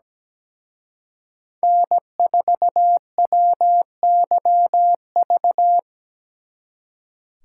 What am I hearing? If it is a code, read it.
Morse code, heard as a keyed tone, about 20 wpm, 700 Hz. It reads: N4WYV
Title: Speed Racing - USA Call Signs 17wpm